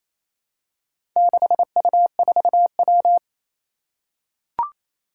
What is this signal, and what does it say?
Morse code, heard as a keyed tone, about 29 wpm, 700 Hz. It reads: 6U4W
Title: Four Challenging Letter-Number Combinations 28wpm